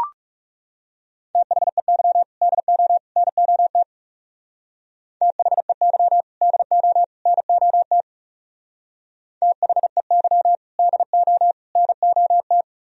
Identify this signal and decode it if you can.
Morse code, heard as a keyed tone, about 36 wpm, 700 Hz. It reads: THEY DO NOT THEY DO NOT THEY DO NOT
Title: Speed-Racing - Sets of 3 Words 30wpm